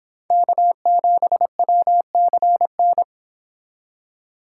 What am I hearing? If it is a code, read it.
Morse code, heard as a keyed tone, about 31 wpm, 700 Hz. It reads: K7WCN